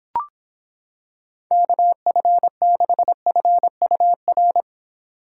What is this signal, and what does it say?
Morse code, heard as a keyed tone, about 26 wpm, 700 Hz. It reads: KF6FUR